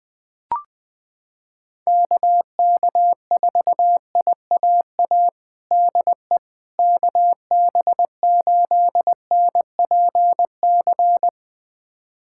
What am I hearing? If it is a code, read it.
Morse code, heard as a keyed tone, about 21 wpm, 700 Hz. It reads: KK4IAA DE KB8NPC
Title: USA Callsign DE Callsign - 6 Characters - 20wpm